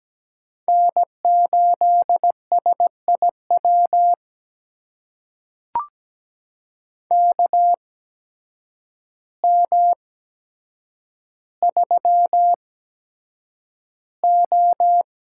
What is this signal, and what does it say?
Morse code, heard as a keyed tone, about 17 wpm, spaced as for 8 wpm, 700 Hz. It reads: N8SIW KM3O
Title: USA Call Signs - 8x Character Spacing - 17wpm